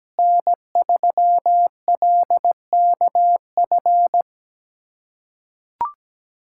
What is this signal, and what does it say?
Morse code, heard as a keyed tone, about 17 wpm, 700 Hz. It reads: N3LKF
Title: USA Callsign DE Callsign - 5 Characters - 17wpm